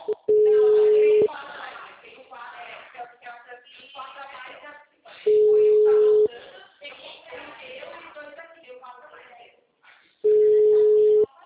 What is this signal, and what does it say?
Ringback tone (Europe, 425 Hz), with speech.